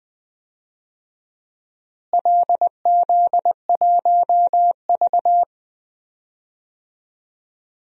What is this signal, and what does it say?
Morse code, heard as a keyed tone, about 20 wpm, 700 Hz. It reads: LZ1V
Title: DXCC-100 Call Signs - 4 Characters - Repeated 2x - 20wpm